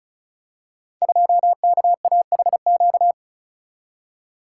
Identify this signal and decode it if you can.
Morse code, heard as a keyed tone, about 34 wpm, 700 Hz. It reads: 2KAHQ
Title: Code Groups - 5 Letter:Numbers - 35wpm